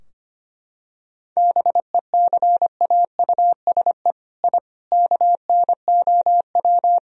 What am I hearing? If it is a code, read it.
Morse code, heard as a keyed tone, about 25 wpm, 700 Hz. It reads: BECAUSE I KNOW
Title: Sentences from Top 1000 Words - Review Part 3 - 25wpm